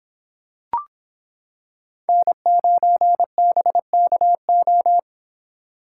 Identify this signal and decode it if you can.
Morse code, heard as a keyed tone, about 26 wpm, 700 Hz. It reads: N9BKO